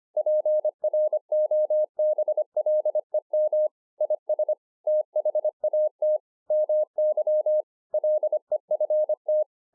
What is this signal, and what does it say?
Morse code, heard as a keyed tone, about 25 wpm, 610 Hz. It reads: PROBLEM IS THAT MY LEFT